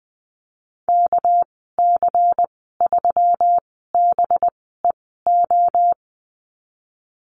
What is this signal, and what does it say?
Morse code, heard as a keyed tone, about 20 wpm, spaced as for 14 wpm, 700 Hz. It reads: KC3BEO
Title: 213 Lesson 34g - Number 8 - Call Signs 20wpm